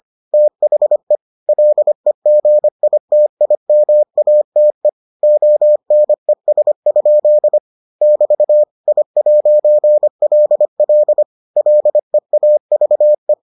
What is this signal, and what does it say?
Morse code, heard as a keyed tone, about 25 wpm, 600 Hz. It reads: THE LEGITIMATE ONES? <BT> I'LL LEAVE